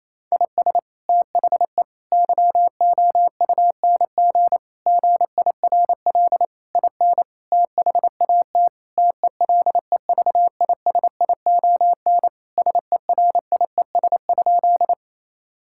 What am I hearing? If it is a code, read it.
Morse code, heard as a keyed tone, about 27 wpm, 700 Hz. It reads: IS THE YOUNG GIRL IN THAT TELEVISION SERIES?
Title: Sentences from Top 1000 Words - Review Part 5 - 28wpm